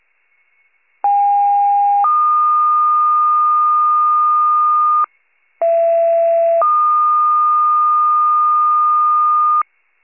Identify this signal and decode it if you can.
Two-tone page, heard as a one-second tone, then a three-second tone, about 810 then 1220 Hz, about 670 then 1220 Hz. It is heard from 1.0 to 9.6 s.